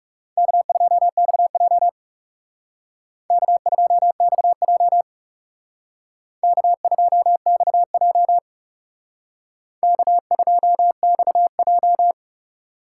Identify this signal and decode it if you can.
Morse code, heard as a keyed tone, about 38 wpm, 700 Hz. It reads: K2XJ K2XJ K2XJ K2XJ